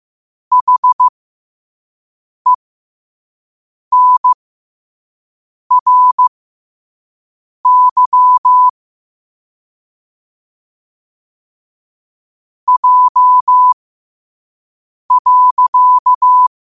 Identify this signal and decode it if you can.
Morse code, heard as a keyed tone, about 15 wpm, spaced as for 5 wpm, 1000 Hz. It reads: HENRY J.
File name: qotdfiles / 5wpm